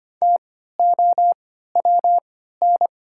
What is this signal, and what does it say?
Morse code, heard as a keyed tone, about 25 wpm, spaced as for 14 wpm, 700 Hz. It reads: TOWN
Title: 3x Character Spacing - Top 500 Words - 25wpm